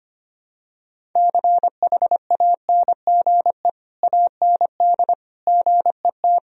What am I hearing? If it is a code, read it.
Morse code, heard as a keyed tone, about 25 wpm, 700 Hz. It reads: CHANGE AND GET
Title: Sentences from Top 1000 Words - Review Part 1 - 25wpm